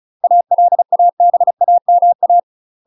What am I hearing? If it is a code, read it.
Morse code, heard as a keyed tone, about 36 wpm, 700 Hz. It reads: ALABAMA